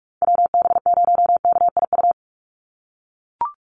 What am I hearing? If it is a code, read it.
Morse code, heard as a keyed tone, about 44 wpm, 700 Hz. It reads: WB0KIU
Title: NA SOTA Chaser Callsigns 45wpm